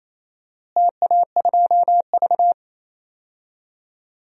Morse code, heard as a keyed tone, about 27 wpm, 700 Hz. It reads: TA2V
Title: DXCC-100 Call Signs - 4 Characters - Repeated 2x - 28wpm